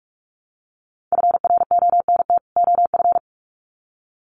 Morse code, heard as a keyed tone, about 44 wpm, 700 Hz. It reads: FRONT OF
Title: Sets of 2 Words 45wpm